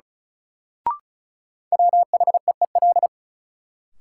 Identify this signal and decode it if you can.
Morse code, heard as a keyed tone, about 36 wpm, 700 Hz. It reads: WHEEL
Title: Sentences from Top 500 Words - New Word Drill - 35wpm